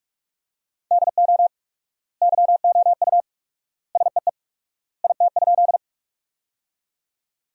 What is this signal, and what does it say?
Morse code, heard as a keyed tone, about 43 wpm, 700 Hz. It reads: DO YOU SEE IT?
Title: 4x Word-Spacing - Sentences from Top 100 Words - 45wpm